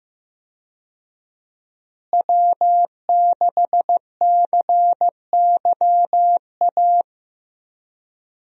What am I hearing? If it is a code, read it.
Morse code, heard as a keyed tone, about 15 wpm, 700 Hz. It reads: W6CYA
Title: USA Call Signs - 5 Characters - Repeated 2x - 15wpm